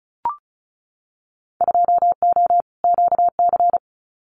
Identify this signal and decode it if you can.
Morse code, heard as a keyed tone, about 34 wpm, 700 Hz. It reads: 2O QC
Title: Winter Field Day Exchanges 35wpm